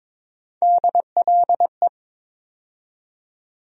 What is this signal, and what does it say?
Morse code, heard as a keyed tone, about 21 wpm, 700 Hz. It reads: DLE
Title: USA Call Sign Postfixes - 3 Letters 22wpm